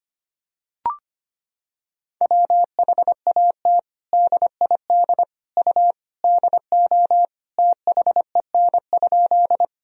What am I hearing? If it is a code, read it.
Morse code, heard as a keyed tone, about 25 wpm, 700 Hz. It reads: WHAT DID U DO THEN?
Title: Sentences from Top 100 Words and Abbreviations 25wpm